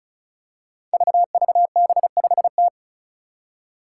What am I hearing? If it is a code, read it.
Morse code, heard as a keyed tone, about 34 wpm, 700 Hz. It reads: VVB5T
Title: Five Challenging Letter-Number Combinations 35wpm